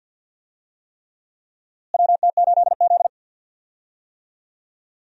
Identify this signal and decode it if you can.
Morse code, heard as a keyed tone, about 46 wpm, 700 Hz. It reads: WT8Z
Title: USA Call Signs - 4 Characters - Repeated 3x - 50wpm